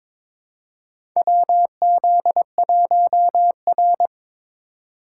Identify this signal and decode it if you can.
Morse code, heard as a keyed tone, about 22 wpm, 700 Hz. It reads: WZ1R
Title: USA Call Signs - 3x Character Spacing - 22wpm